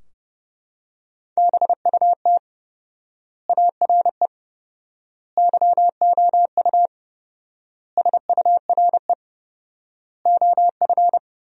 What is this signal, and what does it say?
Morse code, heard as a keyed tone, about 30 wpm, 700 Hz. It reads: BUT ARE YOU SURE OF
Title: 4x Word-Spacing - Sentences from Top 300 Words - 30wpm